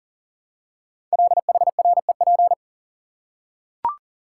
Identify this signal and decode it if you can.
Morse code, heard as a keyed tone, about 40 wpm, 700 Hz. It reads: LHREP
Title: Code Groups - 5 Letters - 40wpm